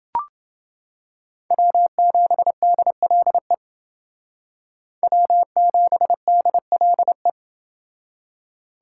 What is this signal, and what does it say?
Morse code, heard as a keyed tone, about 29 wpm, 700 Hz. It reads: W7DLE W7DLE